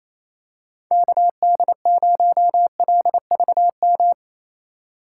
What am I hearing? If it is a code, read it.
Morse code, heard as a keyed tone, about 28 wpm, 700 Hz. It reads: KD0LVM
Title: USA Call Signs - 6 Characters - 4x Character Spacing - 28wpm